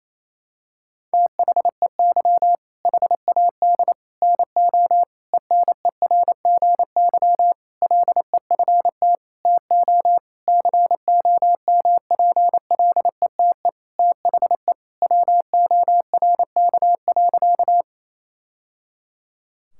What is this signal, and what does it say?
Morse code, heard as a keyed tone, about 28 wpm, 700 Hz. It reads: THEY HAD NO ENERGY LEFT TO COMPLETE THE WORK.
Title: Sentences from Top 1000 Words - Review Part 6 - 28wpm